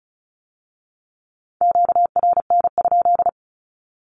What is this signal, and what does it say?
Morse code, heard as a keyed tone, about 36 wpm, 700 Hz. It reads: QRN?
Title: Q-Codes with Spoken Meaning - 35wpm